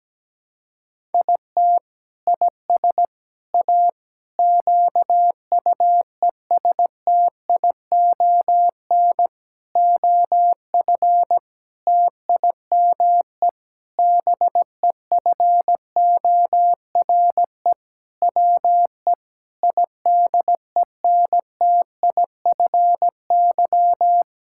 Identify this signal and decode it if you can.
Morse code, heard as a keyed tone, about 17 wpm, 700 Hz. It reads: IT IS A QUESTION OF TIME BEFORE WE IDENTIFY
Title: Sentences from Top 1000 Words - Review Part 6 - 17wpm